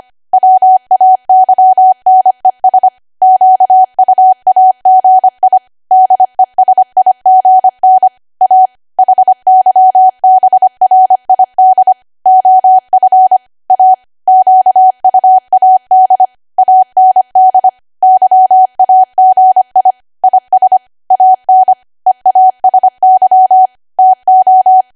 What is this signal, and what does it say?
Morse code, heard as a keyed tone, about 25 wpm, 750 Hz. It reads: WAYNES QUAGI DESIGN A HYBRID OF A QUAD AND YAGI IS AN EASY TO
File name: Archive / 25 WPM / 25WPM